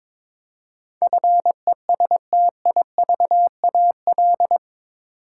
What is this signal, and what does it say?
Morse code, heard as a keyed tone, about 22 wpm, 700 Hz. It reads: FESTIVAL
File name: practice / 22wpm / Eight Letter Words - 22wpm